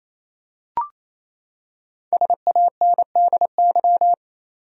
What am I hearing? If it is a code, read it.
Morse code, heard as a keyed tone, about 28 wpm, 700 Hz. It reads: SANDY
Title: USA Common Names - Max 5 Letters - 28wpm